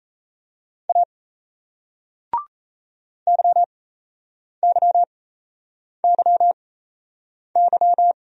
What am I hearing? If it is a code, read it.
Morse code, heard as a keyed tone, about 37 wpm, spaced as for 8 wpm, 700 Hz. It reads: A YYYY